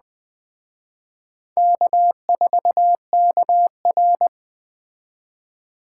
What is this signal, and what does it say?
Morse code, heard as a keyed tone, about 20 wpm, 700 Hz. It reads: K4KR
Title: Speed Racing - USA 1x2 Call Signs 17wpm